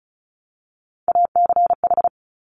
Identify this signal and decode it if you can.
Morse code, heard as a keyed tone, about 36 wpm, 690 Hz. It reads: ACH
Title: Top Three Letter Combinations in Words 35wpm